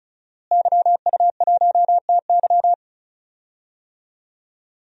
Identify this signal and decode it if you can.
Morse code, heard as a keyed tone, about 36 wpm, 700 Hz. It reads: YU1TY
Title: DXCC-100 Call Signs - 5 Characters - Repeated 2x - 35wpm